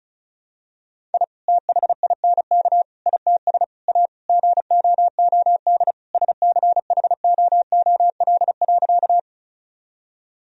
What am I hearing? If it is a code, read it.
Morse code, heard as a keyed tone, about 36 wpm, 700 Hz. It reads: I THINK ITS A GOOD SCHOOL.
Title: Sentences from Top 1000 Words - Review Part 1 - 35wpm